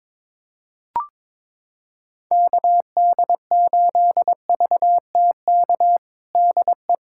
Morse code, heard as a keyed tone, about 22 wpm, 700 Hz. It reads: KD8VTK DE
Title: USA Callsign DE Callsign 22wpm